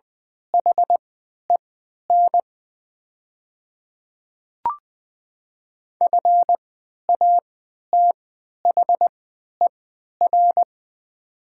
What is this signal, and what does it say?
Morse code, heard as a keyed tone, about 20 wpm, spaced as for 11 wpm, 700 Hz. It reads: HEN FATHER
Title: 3x Character Spacing - Top 500 Words - 20wpm